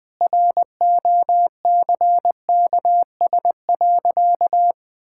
Morse code, heard as a keyed tone, about 21 wpm, 700 Hz. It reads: ROCKS.
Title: Pangram Challenge v2 - 20wpm